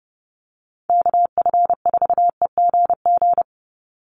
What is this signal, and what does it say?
Morse code, heard as a keyed tone, about 30 wpm, 700 Hz. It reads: KF4EGG
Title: USA Call Signs 30wpm